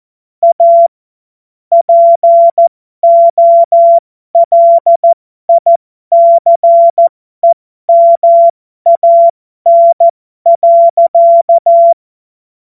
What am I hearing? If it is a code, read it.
Morse code, heard as a keyed tone, about 14 wpm, spaced as for 12 wpm, 680 Hz. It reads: A POLICEMAN.
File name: Morsekurs / Witze